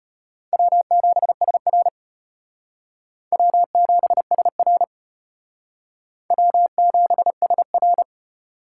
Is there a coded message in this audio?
Yes: Morse code from 0.5 to 8.0 s, heard as a keyed tone, about 34 wpm, 700 Hz. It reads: W7SR W7SR W7SR